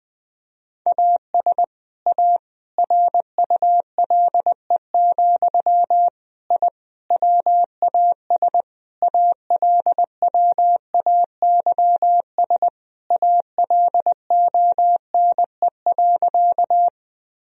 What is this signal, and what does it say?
Morse code, heard as a keyed tone, about 21 wpm, 700 Hz. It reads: AS A RULE, I WAS ALWAYS ALONE.
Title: Sentences from Top 1000 Words - Review Part 2 - 20wpm